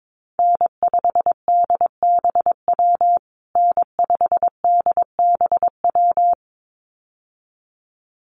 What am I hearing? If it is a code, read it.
Morse code, heard as a keyed tone, about 22 wpm, 700 Hz. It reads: N5DBW N5DBW
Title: USA Call Signs Repeated 2x 22wpm